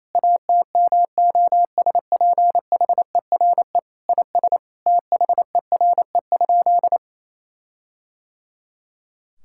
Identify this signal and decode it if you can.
Morse code, heard as a keyed tone, about 28 wpm, 700 Hz. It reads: ATMOSPHERE IS THERE?
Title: Sentences from Top 1600 Words 28wpm